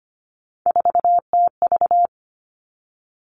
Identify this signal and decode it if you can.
Morse code, heard as a keyed tone, about 25 wpm, 700 Hz. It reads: 4TV
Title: Three Challenging Letter-Number Combinations 25wpm